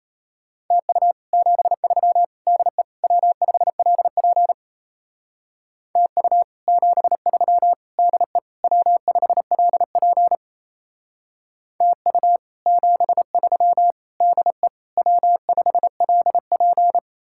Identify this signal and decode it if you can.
Morse code, heard as a keyed tone, about 34 wpm, 700 Hz. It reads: TU 73 DE W5LP TU 73 DE W5LP TU 73 DE W5LP
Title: Speed-Racing Common QSO Phrases - 28wpm